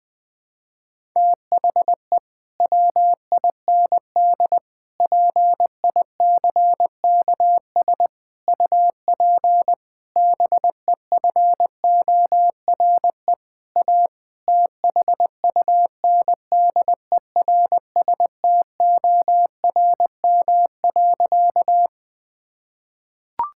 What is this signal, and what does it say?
Morse code, heard as a keyed tone, about 21 wpm, 700 Hz. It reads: THE WIND PICKS UP BEFORE A THUNDERSTORM.
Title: Difficult Sentences with Hints - 20wpm